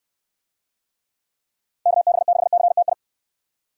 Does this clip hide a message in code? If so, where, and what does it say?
Yes: Morse code from 1.9 to 2.9 s, heard as a keyed tone, about 61 wpm, 700 Hz. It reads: KB6CTI